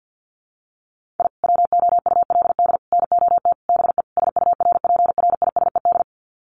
Morse code, heard as a keyed tone, about 48 wpm, 700 Hz. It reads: I WOULD NOT BE SURPRISED
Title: Sets of 5 Words Encore2 50wpm